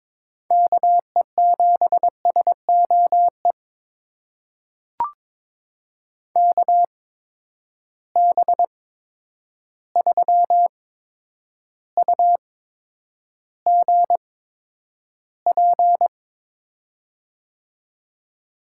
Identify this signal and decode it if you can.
Morse code, heard as a keyed tone, about 22 wpm, spaced as for 9 wpm, 700 Hz. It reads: KE7SOE KB3UGP
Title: USA Call Signs - 6 Characters - 8x Character Spacing - 22wpm